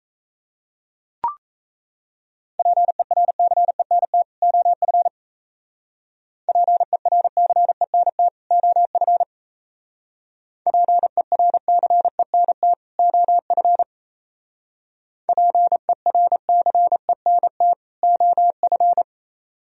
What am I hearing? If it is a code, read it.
Morse code, heard as a keyed tone, about 34 wpm, 700 Hz. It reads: PERCENT OF PERCENT OF PERCENT OF PERCENT OF
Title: Speed-Racing - Sets of 2 Words 28wpm